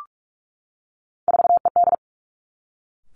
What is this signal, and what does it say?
Morse code, heard as a keyed tone, about 46 wpm, 700 Hz. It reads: 4ED
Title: Three Challenging Letter-Number Combinations 45wpm